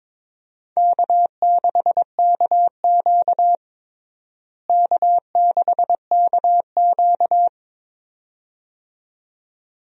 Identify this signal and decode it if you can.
Morse code, heard as a keyed tone, about 21 wpm, 700 Hz. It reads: K6KQ K6KQ
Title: USA Call Signs - 4 Characters - Repeated 2x - 22wpm